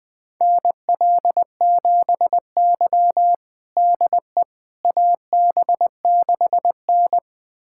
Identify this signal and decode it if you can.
Morse code, heard as a keyed tone, about 20 wpm, 700 Hz. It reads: NL7Y DE AB6N
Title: USA Callsign DE Callsign - 4 Characters - 20wpm